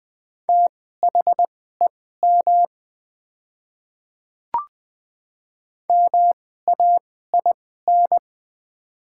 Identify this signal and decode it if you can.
Morse code, heard as a keyed tone, about 20 wpm, spaced as for 15 wpm, 700 Hz. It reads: THEM MAIN
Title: 2x Character Spacing - Top 300 Words - 20wpm